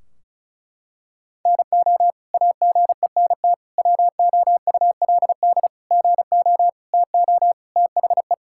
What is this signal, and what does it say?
Morse code, heard as a keyed tone, about 36 wpm, 700 Hz. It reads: NO AGENT WOULD GO TO THE
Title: Sentences from Top 1600 Words 35wpm